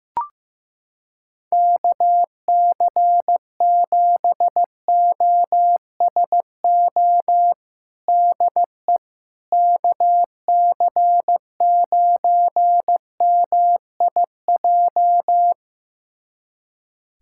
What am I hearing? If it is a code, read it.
Morse code, heard as a keyed tone, about 16 wpm, 700 Hz. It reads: KC7OSO DE KC9MIJ
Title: USA Callsign DE Callsign - 6 Characters - 15wpm